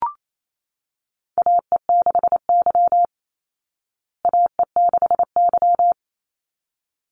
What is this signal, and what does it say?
Morse code, heard as a keyed tone, about 28 wpm, 700 Hz. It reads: AE6Y AE6Y
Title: Call Signs Repeated 3x - 28wpm